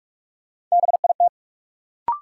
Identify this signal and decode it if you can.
Morse code, heard as a keyed tone, about 43 wpm, 700 Hz. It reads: BIT
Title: Sentences from Top 900 Words - New Word Drill - 45wpm